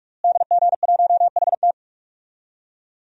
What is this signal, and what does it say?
Morse code, heard as a keyed tone, about 46 wpm, 700 Hz. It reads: DG1HT
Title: DXCC-100 Call Signs - 5 Characters - Repeated 3x - 45wpm